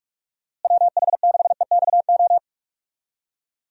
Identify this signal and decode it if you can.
Morse code, heard as a keyed tone, about 46 wpm, 700 Hz. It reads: WH6EXO
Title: USA Call Signs - 6 Characters - Repeated 3x - 45wpm